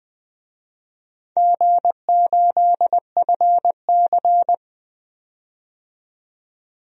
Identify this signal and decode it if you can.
Morse code, heard as a keyed tone, about 20 wpm, 700 Hz. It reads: G8FC
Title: DXCC-100 Call Signs - 4 Characters - 20wpm